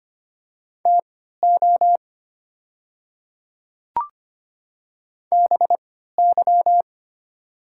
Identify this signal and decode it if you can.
Morse code, heard as a keyed tone, about 25 wpm, spaced as for 14 wpm, 700 Hz. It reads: TO BY
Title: 3x Character Spacing - Top 100 Words - 25wpm